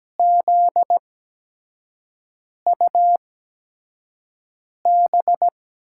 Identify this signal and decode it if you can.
Morse code, heard as a keyed tone, about 17 wpm, spaced as for 5 wpm, 700 Hz. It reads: ZUB
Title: USA Call Signs - 5 Characters - 8x Character Spacing - 17wpm